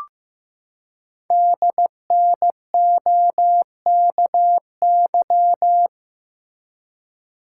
Morse code, heard as a keyed tone, about 15 wpm, 700 Hz. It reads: DNOKY
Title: Code Groups - 5 Letters - 15wpm